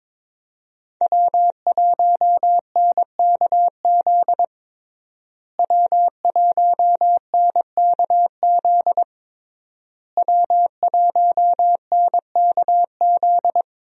Morse code, heard as a keyed tone, about 22 wpm, 700 Hz. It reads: W1NKZ W1NKZ W1NKZ
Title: USA Call Signs - 5 Characters - Repeated 3x - 22wpm